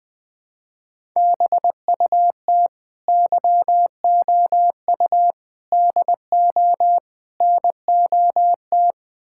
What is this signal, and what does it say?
Morse code, heard as a keyed tone, about 20 wpm, 690 Hz. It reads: BUT YOU DO NOT
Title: Build A Phrase - 4-Word Phrases - 20wpm